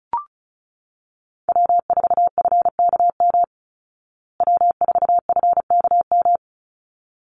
Morse code, heard as a keyed tone, about 34 wpm, 700 Hz. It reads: W4FKM W4FKM
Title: Call Signs Repeated 3x - 35wpm